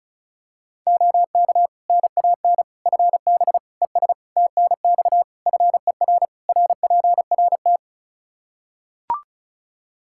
Morse code, heard as a keyed tone, about 36 wpm, 700 Hz. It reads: OK NAN FB ES TNX FER RPRT
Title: Practice QSO - Exchange 2 - Info - 35wpm